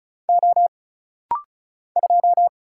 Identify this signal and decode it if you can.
Morse code, heard as a keyed tone, about 34 wpm, 700 Hz. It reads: O 2
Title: Single Letter-Number Alt - Rapid-Fire 35wpm